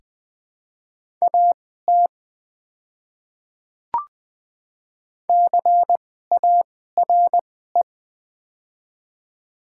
Morse code, heard as a keyed tone, about 20 wpm, spaced as for 14 wpm, 700 Hz. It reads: AT CARE
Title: 2x Character Spacing - Top 300 Words - 20wpm